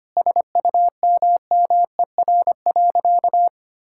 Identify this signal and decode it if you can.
Morse code, heard as a keyed tone, about 25 wpm, 700 Hz. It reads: SUMMER.
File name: practice / 25wpm / Difficult Sentences with Hints - 25wpm